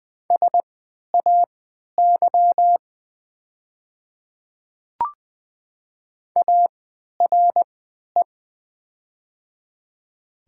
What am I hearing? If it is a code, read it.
Morse code, heard as a keyed tone, about 20 wpm, spaced as for 11 wpm, 700 Hz. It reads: SAY ARE
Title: 3x Character Spacing - Top 500 Words - 20wpm